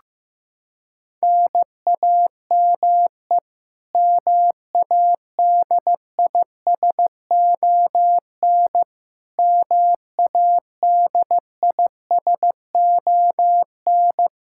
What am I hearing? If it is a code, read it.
Morse code, heard as a keyed tone, about 15 wpm, 700 Hz. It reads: NAME MADISON MADISON
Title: K1USN SST QSO Practice 15wpm